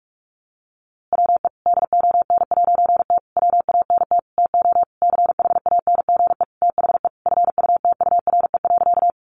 Morse code, heard as a keyed tone, about 44 wpm, 700 Hz. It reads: WE DON'T WANT TO CHANGE THE FUTURE.
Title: A Sound Of Thunder 45wpm